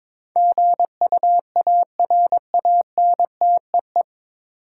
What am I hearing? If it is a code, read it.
Morse code, heard as a keyed tone, about 22 wpm, 700 Hz. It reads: GUARANTEE
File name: 9 Letter Words - First 3 Letters Given - 22wpm